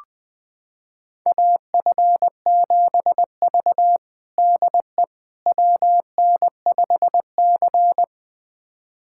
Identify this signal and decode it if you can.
Morse code, heard as a keyed tone, about 20 wpm, 700 Hz. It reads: AF7V DE WN5C
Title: USA Callsign DE Callsign - 4 Characters - 20wpm